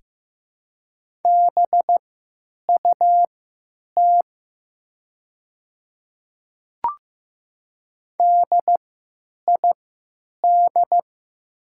Morse code, heard as a keyed tone, about 15 wpm, spaced as for 9 wpm, 700 Hz. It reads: BUT DID
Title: 3x Character Spacing - Top 100 Words - 15wpm